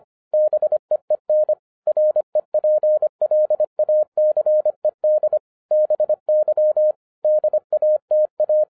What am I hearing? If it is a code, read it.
Morse code, heard as a keyed tone, about 26 wpm, 610 Hz. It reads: BEEN REPLACED BY DATA